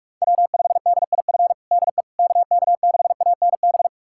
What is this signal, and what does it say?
Morse code, heard as a keyed tone, about 46 wpm, 700 Hz. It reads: W5DIF DE KK6ANB
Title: USA Callsign DE Callsign 45wpm